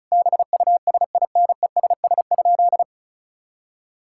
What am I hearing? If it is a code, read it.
Morse code, heard as a keyed tone, about 36 wpm, 700 Hz. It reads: BUSINESS?
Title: Sentences from Top 1000 Words - Review Part 6 - 35wpm